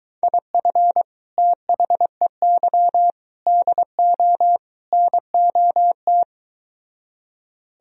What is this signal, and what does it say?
Morse code, heard as a keyed tone, about 22 wpm, 700 Hz. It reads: IF THEY DO NOT